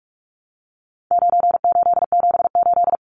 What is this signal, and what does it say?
Morse code, heard as a keyed tone, about 44 wpm, 700 Hz. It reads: 9878
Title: Numbers - 4-Digit 45wpm